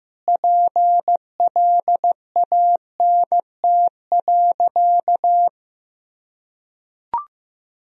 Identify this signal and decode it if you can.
Morse code, heard as a keyed tone, about 15 wpm, 700 Hz. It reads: PLANT.
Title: Sentences from Top 1000 Words - Review Part 1 - 15wpm